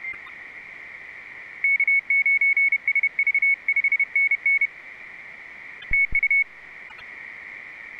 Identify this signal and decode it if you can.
Morse code, heard as a keyed tone, about 30 wpm, 2200 Hz. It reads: K9SV 5NN TU